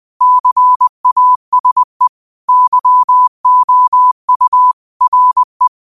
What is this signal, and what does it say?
Morse code, heard as a keyed tone, about 20 wpm, 1000 Hz. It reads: CASE YOU RE
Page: Daily quotes in Morse Code at 20 Words per Minute. Use this to improve your code copying skill.